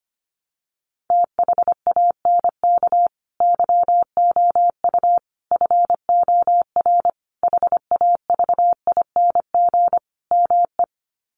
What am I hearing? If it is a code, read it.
Morse code, heard as a keyed tone, about 25 wpm, 700 Hz. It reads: THANK YOU FOR HAVING ME
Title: Sets of 5 Words Encore 25wpm